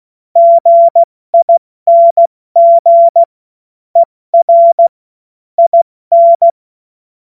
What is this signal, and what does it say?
Morse code, heard as a keyed tone, about 16 wpm, 680 Hz. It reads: GING ER IN
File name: Morsekurs / Witze / Witz 12 März 2024 70 F80